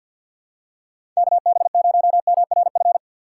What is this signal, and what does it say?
Morse code, heard as a keyed tone, about 53 wpm, 700 Hz. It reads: KB0MRF
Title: USA Call Signs - 6 Characters - 3x Character Spacing - 50wpm